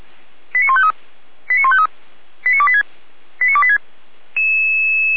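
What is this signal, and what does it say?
Five-tone selective call (ZVEI), heard as quick runs of five tones, tones of about 80 ms. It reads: 89153 89153 89286 89286
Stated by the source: ZVEI five-tone call received by SDR, fire brigade Orlamünde (feuerwehr-orlamuende) recording.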